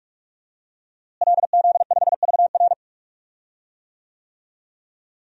Morse code, heard as a keyed tone, about 46 wpm, 700 Hz. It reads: LZ5VR
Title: DXCC-100 Call Signs - 5 Characters - Repeated 2x - 45wpm